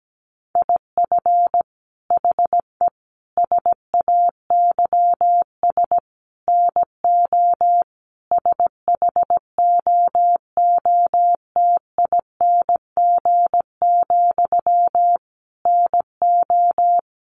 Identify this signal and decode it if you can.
Morse code, heard as a keyed tone, about 17 wpm, 700 Hz. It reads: IF HE SAYS NO SHOOTING, NO
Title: A Sound Of Thunder 17wpm